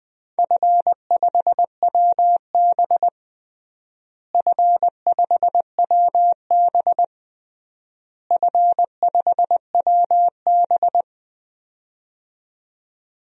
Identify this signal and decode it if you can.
Morse code, heard as a keyed tone, about 20 wpm, 700 Hz. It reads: F5WB F5WB F5WB
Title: DXCC-100 Call Signs - 4 Characters - Repeated 3x - 20wpm